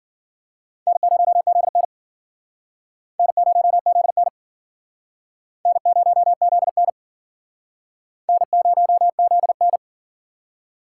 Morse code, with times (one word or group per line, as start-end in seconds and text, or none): 0.9-1.8 N0ZN
3.2-4.3 N0ZN
5.7-6.9 N0ZN
8.3-9.8 N0ZN